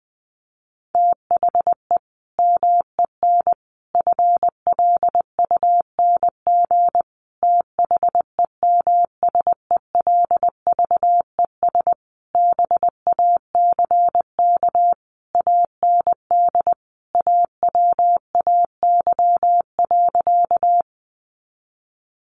Morse code, heard as a keyed tone, about 20 wpm, 700 Hz. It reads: THE MEN FLUNG THEMSELVES BACK AND AWAY.
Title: A Sound Of Thunder 20wpm